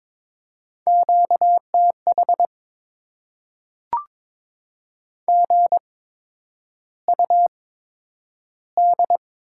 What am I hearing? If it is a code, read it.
Morse code, heard as a keyed tone, about 22 wpm, spaced as for 9 wpm, 700 Hz. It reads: QTH GUD
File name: Top 20 QSO Elements - 8x Character Spacing - 22wpm